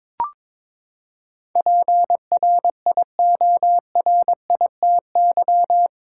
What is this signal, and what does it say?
Morse code, heard as a keyed tone, about 22 wpm, 700 Hz. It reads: PRIORITY
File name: practice / 22wpm / Eight Letter Words - 22wpm